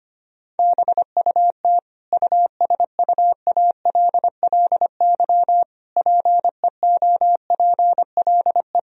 Morse code, heard as a keyed tone, about 25 wpm, 700 Hz. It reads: BUT USUALLY PEOPLE
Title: Sentences from Top 1000 Words - Review Part 3 - 25wpm